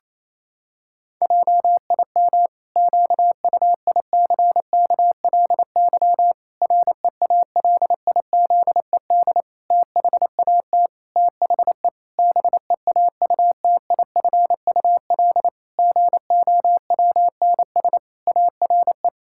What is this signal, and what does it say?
Morse code, heard as a keyed tone, about 28 wpm, 700 Hz. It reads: JIM QUICKLY REALIZED THAT THE BEAUTIFUL GOWNS ARE
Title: Pangram Challenge v2 - 28wpm